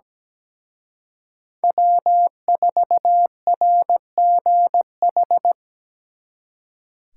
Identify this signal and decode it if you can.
Morse code, heard as a keyed tone, about 20 wpm, 700 Hz. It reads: W4RGH
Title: Speed Racing - USA 1x3 Call Signs 17wpm